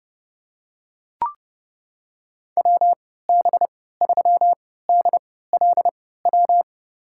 Morse code, heard as a keyed tone, about 32 wpm, spaced as for 17 wpm, 700 Hz. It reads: WB3DLW
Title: USA Call Signs - 3x Character Spacing - 30wpm